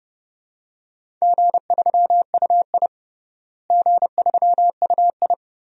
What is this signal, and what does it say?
Morse code, heard as a keyed tone, about 30 wpm, 700 Hz. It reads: G3UI G3UI
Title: DXCC-100 Call Signs - 4 Characters - Repeated 2x - 30wpm